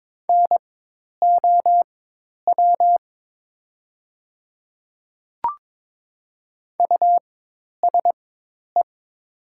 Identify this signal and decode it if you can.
Morse code, heard as a keyed tone, about 22 wpm, spaced as for 10 wpm, 700 Hz. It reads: NOW USE
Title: 4x Character Spacing - Top 100 Words - 22wpm